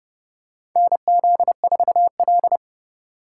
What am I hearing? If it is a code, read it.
Morse code, heard as a keyed tone, about 30 wpm, 700 Hz. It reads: NZ4L
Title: USA Call Signs - 4 Characters - 4x Character Spacing - 30wpm